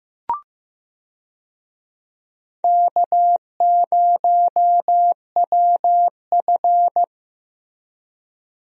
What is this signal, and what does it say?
Morse code, heard as a keyed tone, about 15 wpm, 700 Hz. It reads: K0WF
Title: USA Call Signs - 4 Characters - Repeated 3x - 15wpm